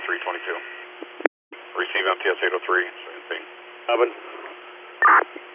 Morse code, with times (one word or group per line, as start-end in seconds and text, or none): none